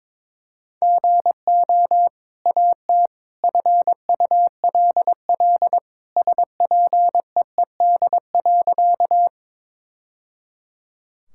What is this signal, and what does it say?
Morse code, heard as a keyed tone, about 22 wpm, 700 Hz. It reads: GO AT FULL SPEED.
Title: Sentences from Top 1000 Words - Review Part 5 - 22wpm